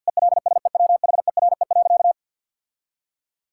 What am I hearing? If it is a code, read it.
Morse code, heard as a keyed tone, about 50 wpm, 700 Hz. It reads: ELSEWHERE.
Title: Sentences from Top 1600 Words 50wpm